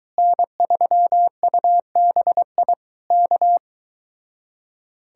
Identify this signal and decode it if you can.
Morse code, heard as a keyed tone, about 23 wpm, 700 Hz. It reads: N3UBI K